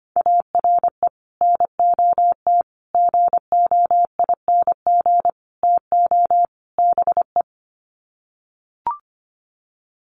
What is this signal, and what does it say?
Morse code, heard as a keyed tone, about 25 wpm, 700 Hz. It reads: ARE NOT GOING TO BE
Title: Sets of 5 Words Encore 25wpm